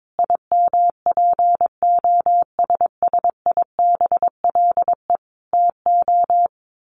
Morse code, heard as a keyed tone, about 22 wpm, 700 Hz. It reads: IMPOSSIBLE TO
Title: Sentences from Top 1200 Words 22wpm